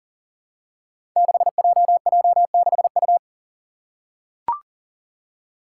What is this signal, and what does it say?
Morse code, heard as a keyed tone, about 43 wpm, 700 Hz. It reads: 6JJ6U
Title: Five Challenging Letter-Number Combinations 40wpm